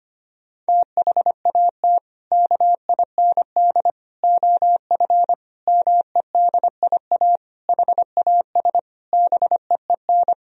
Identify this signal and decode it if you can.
Morse code, heard as a keyed tone, about 25 wpm, 700 Hz. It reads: THAT KIND OF MEDIA HAS BEEN
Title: Sentences from Top 1000 Words - Review Part 4 - 25wpm